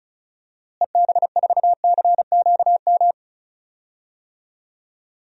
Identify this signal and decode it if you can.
Morse code, heard as a keyed tone, about 34 wpm, 700 Hz. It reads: EB4CQM
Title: DXCC-100 Call Signs - 6 Characters - Repeated 3x - 35wpm